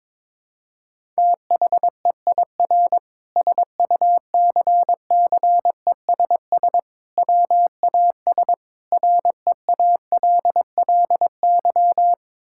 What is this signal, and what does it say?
Morse code, heard as a keyed tone, about 22 wpm, 700 Hz. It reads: THEIR SUCCESS WAS REALLY
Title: Sentences from Top 1000 Words - Review Part 6 - 22wpm